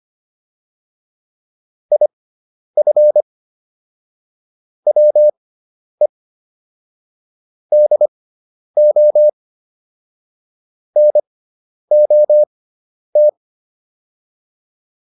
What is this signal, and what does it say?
Morse code, heard as a keyed tone, about 25 wpm, spaced as for 10 wpm, 600 Hz. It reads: IF WE DO NOT